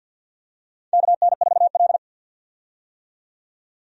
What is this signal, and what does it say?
Morse code, heard as a keyed tone, about 49 wpm, 700 Hz. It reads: KN4L
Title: USA Call Signs - 4 Characters - 50wpm